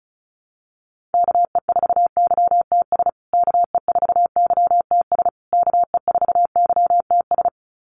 Morse code, heard as a keyed tone, about 34 wpm, 700 Hz. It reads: KE4YTS KE4YTS KE4YTS
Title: USA Call Signs Repeated 3x 35wpm